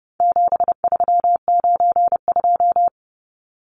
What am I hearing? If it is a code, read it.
Morse code, heard as a keyed tone, about 30 wpm, 700 Hz. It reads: 7392
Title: Numbers - 4-Digit 30wpm